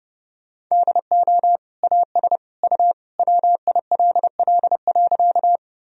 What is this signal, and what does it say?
Morse code, heard as a keyed tone, about 30 wpm, 700 Hz. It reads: DO AS U WILL.
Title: Sentences from Top 100 Words and Abbreviations 30wpm